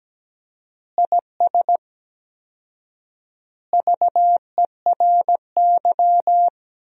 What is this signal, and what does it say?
Morse code, heard as a keyed tone, about 17 wpm, 700 Hz. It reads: IS VERY
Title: 4x Word-Spacing - Sentences from Top 400 Words - 17wpm